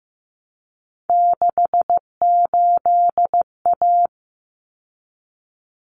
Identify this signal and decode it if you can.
Morse code, heard as a keyed tone, about 15 wpm, 700 Hz. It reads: 68A
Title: Common Cut Numbers - 3-Digit 15wpm